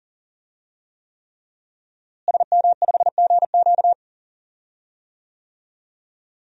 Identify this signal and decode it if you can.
Morse code, heard as a keyed tone, about 39 wpm, 700 Hz. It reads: SM5GQ
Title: DXCC-100 Call Signs - 5 Characters - Repeated 3x - 40wpm